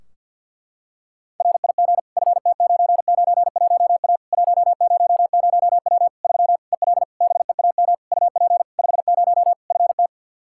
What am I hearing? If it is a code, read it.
Morse code, heard as a keyed tone, about 48 wpm, 700 Hz. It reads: RIG FT991A 100W 3 EL BEAM UP 50 FT
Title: K1USN SST QSO Practice 50wpm